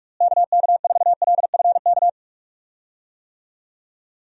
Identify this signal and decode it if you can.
Morse code, heard as a keyed tone, about 46 wpm, 700 Hz. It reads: KK4LFK
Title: USA Call Signs - 6 Characters - Repeated 3x - 45wpm